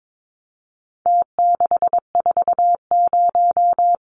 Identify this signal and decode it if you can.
Morse code, heard as a keyed tone, about 22 wpm, 700 Hz. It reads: T640
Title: Numbers - Contest Style - 4-Digit 22wpm